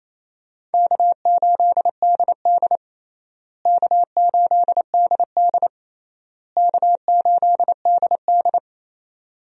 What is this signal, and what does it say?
Morse code, heard as a keyed tone, about 28 wpm, 700 Hz. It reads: K8DD K8DD K8DD
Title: USA Call Signs - 4 Characters - Repeated 3x - 28wpm